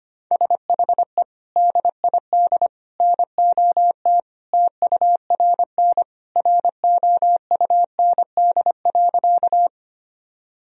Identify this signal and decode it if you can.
Morse code, heard as a keyed tone, about 25 wpm, 700 Hz. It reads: SHE DID NOT TURN ROUND.
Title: Sentences from Top 1000 Words - Review Part 1 - 25wpm